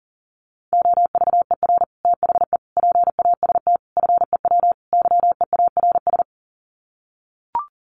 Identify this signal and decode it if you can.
Morse code, heard as a keyed tone, about 40 wpm, 700 Hz. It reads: OVER THE PAST FEW YEARS
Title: Sets of 5 Words Encore 40wpm